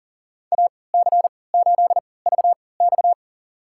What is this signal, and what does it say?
Morse code, heard as a keyed tone, about 37 wpm, spaced as for 22 wpm, 700 Hz. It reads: AC8VX
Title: USA Call Signs - 3x Character Spacing - 40wpm